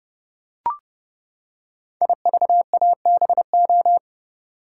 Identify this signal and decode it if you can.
Morse code, heard as a keyed tone, about 30 wpm, 700 Hz. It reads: IVABO
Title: Code Groups - 5 Letters - 30wpm